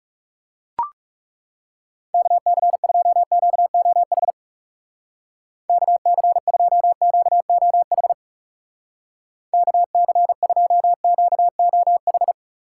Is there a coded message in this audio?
Yes: Morse code from 2.1 to 12.3 s, heard as a keyed tone, about 40 wpm, 700 Hz. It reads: KC2QOH KC2QOH KC2QOH